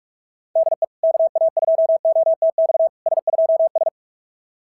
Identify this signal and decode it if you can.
Morse code, heard as a keyed tone, about 44 wpm, 660 Hz. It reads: DE KA2OTX S2S
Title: SOTA QSO Practice 45wpm